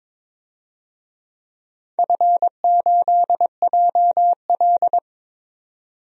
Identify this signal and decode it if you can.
Morse code, heard as a keyed tone, about 22 wpm, 690 Hz. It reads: F8JL
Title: DXCC-100 Call Signs - 4 Characters - 22wpm